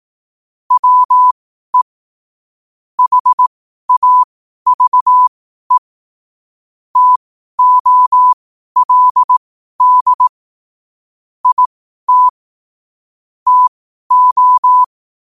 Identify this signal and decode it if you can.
Morse code, heard as a keyed tone, about 18 wpm, spaced as for 12 wpm, 1000 Hz. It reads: WE HAVE TOLD IT TO
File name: qotdfiles / 12wpm